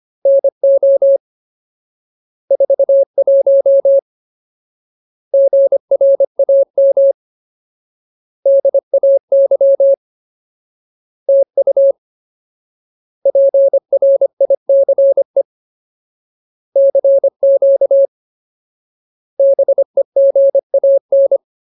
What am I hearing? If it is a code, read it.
Morse code, heard as a keyed tone, about 25 wpm, 550 Hz. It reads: NO 41 GRAM DAY TU PRICE CQ BEGAN
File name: Ultimate Word List / 25WPM Ultimate